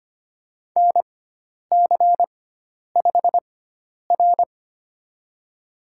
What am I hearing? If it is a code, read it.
Morse code, heard as a keyed tone, about 25 wpm, spaced as for 10 wpm, 700 Hz. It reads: NC5R
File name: Lesson 16g - Letter c - Call Signs 25wpm